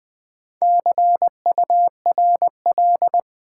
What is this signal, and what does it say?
Morse code, heard as a keyed tone, about 20 wpm, 700 Hz. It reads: CURL
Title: Build A Word - 1 Letter At A Time - 5-Letter Words - 20wpm